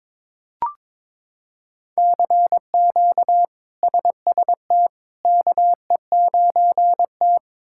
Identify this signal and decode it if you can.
Morse code, heard as a keyed tone, about 22 wpm, 700 Hz. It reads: CQ SST KE9T
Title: K1USN SST QSO Practice - 22wpm